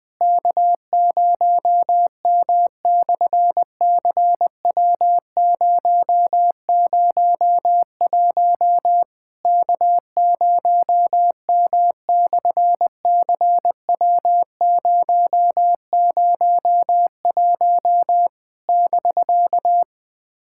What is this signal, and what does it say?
Morse code, heard as a keyed tone, about 18 wpm, 700 Hz. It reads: K0M/CW001 K0M/CW001 <BK>
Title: SOTA QSO Practice 20wpm